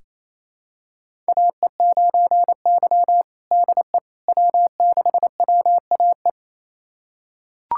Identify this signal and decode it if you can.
Morse code, heard as a keyed tone, about 28 wpm, 700 Hz. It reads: AE9Y DE W6WAE
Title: USA Callsign DE Callsign 28wpm